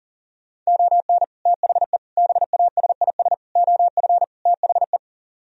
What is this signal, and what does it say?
Morse code, heard as a keyed tone, about 52 wpm, 700 Hz. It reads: ON THE BASIS OF THE